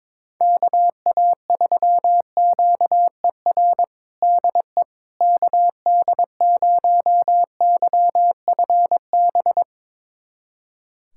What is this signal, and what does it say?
Morse code, heard as a keyed tone, about 22 wpm, 700 Hz. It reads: KA3QER DE KD0YFB
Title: USA Callsign DE Callsign - 6 Characters - 22wpm